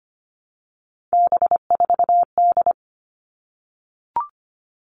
Morse code, heard as a keyed tone, about 25 wpm, 700 Hz. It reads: B4D
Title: All Cut Numbers - 3-Digit 25wpm